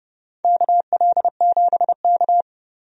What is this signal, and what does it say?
Morse code, heard as a keyed tone, about 30 wpm, 700 Hz. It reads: KL7K
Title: DXCC-100 Call Signs - 4 Characters - 30wpm